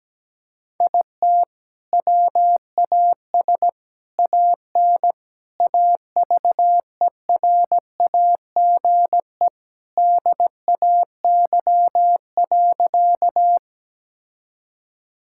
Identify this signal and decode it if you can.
Morse code, heard as a keyed tone, about 17 wpm, 700 Hz. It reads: IT WAS AN AVERAGE DAY.
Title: Sentences from Top 1000 Words - Review Part 6 - 17wpm